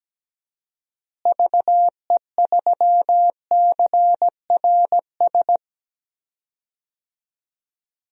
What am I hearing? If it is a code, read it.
Morse code, heard as a keyed tone, about 17 wpm, 700 Hz. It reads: VE3CRS
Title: DXCC-100 Call Signs - 6 Characters - Repeated 2x - 17wpm